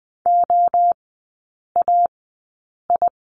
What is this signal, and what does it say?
Morse code, heard as a keyed tone, about 20 wpm, 700 Hz. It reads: O A I
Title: 035 Lesson 7e - Letter s - Three Characters 20wpm